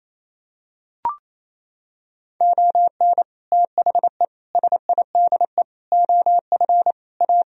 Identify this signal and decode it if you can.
Morse code, heard as a keyed tone, about 28 wpm, 700 Hz. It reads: ON THE SIDE OF A
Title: Sentences from Top 1000 Words - Review Part 3 - 28wpm